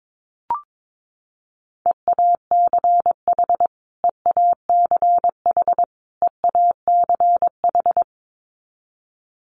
Morse code, heard as a keyed tone, about 22 wpm, 700 Hz. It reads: EACH EACH EACH
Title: Top 300 Words Repeated 22wpm